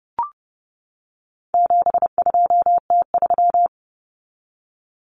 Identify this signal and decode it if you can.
Morse code, heard as a keyed tone, about 30 wpm, 700 Hz. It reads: 72T3
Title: Common Cut Numbers - 4-Digit 30wpm